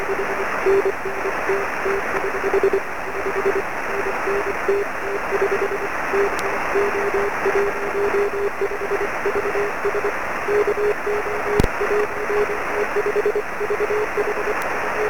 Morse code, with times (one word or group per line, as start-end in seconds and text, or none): none